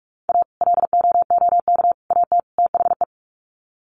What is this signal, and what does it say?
Morse code, heard as a keyed tone, about 44 wpm, 700 Hz. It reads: A LOOK AT THE
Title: Sets of 4 Words 45wpm